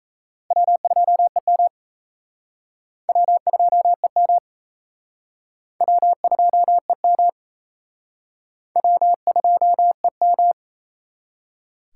Morse code, heard as a keyed tone, about 37 wpm, 700 Hz. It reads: W2EM W2EM W2EM W2EM